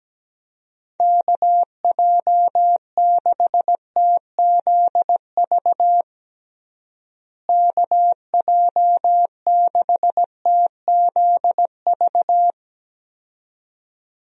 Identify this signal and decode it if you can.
Morse code, heard as a keyed tone, about 17 wpm, 700 Hz. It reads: KJ6TZV KJ6TZV
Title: USA Call Signs - 6 Characters - Repeated 2x - 17wpm